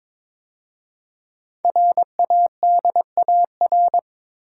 Morse code, heard as a keyed tone, about 22 wpm, 700 Hz. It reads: RADAR
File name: 5 Letter Words - First 3 Letters Given - 22wpm